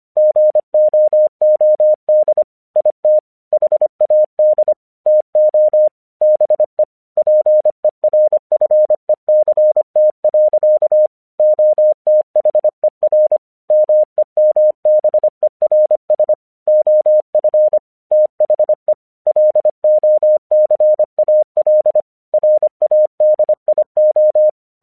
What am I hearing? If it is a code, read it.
Morse code, heard as a keyed tone, about 25 wpm, 610 Hz. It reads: GOOD IT HAD TO BE PERFECT. OTHER MEMBERS OF THE LOCAL RADIO
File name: 1 English 25wpm / AQC11